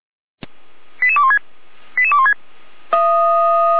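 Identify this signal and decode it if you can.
Five-tone selective call (ZVEI), heard as quick runs of five tones, tones of about 70 ms. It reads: 88216 88216
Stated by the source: ZVEI five-tone selective call received via SDR.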